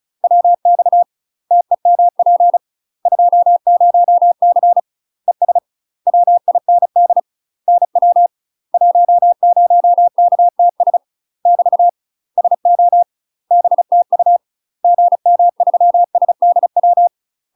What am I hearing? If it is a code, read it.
Morse code, heard as a keyed tone, about 37 wpm, 700 Hz. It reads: WX TEMP 20C ES WIND NW 10KTS <BT> SO BTU GM3SDW
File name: UK MASTER QSOs TRAINING / 35WPM / UK MASTER QSOs 0015